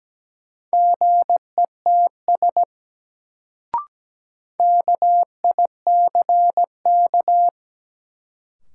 Morse code, heard as a keyed tone, about 17 wpm, 700 Hz. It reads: GETS KICK
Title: Sentences from Top 1300 Words - New Word Drill - 17wpm